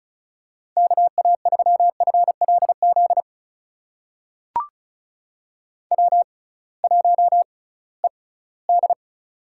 Morse code, heard as a keyed tone, about 35 wpm, spaced as for 20 wpm, 700 Hz. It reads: KA3FLZ W1ED
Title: USA Call Signs - 6x Character Spacing - 35wpm